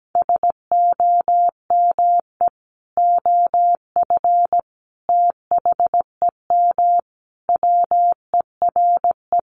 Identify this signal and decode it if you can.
Morse code, heard as a keyed tone, about 17 wpm, 700 Hz. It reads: SOME OF THEM WERE
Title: Sentences from Top 1200 Words 17wpm